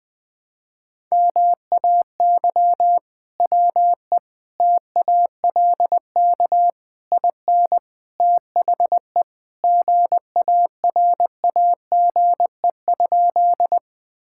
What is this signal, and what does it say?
Morse code, heard as a keyed tone, about 21 wpm, 700 Hz. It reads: MAY WE TALK IN THE GARAGE?
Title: Sentences from Top 1600 Words 20wpm